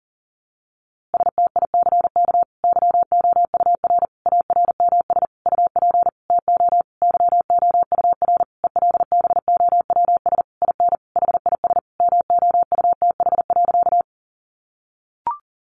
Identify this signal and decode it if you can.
Morse code, heard as a keyed tone, about 40 wpm, 700 Hz. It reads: STICK YOUR ARMS UP TO YOUR ELBOWS IN HIS MOUTH.
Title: A Sound Of Thunder 40wpm